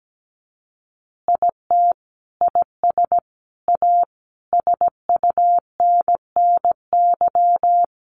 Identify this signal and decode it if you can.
Morse code, heard as a keyed tone, about 17 wpm, 700 Hz. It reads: IT IS A SUNNY
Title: Additional Easy Phrases 17wpm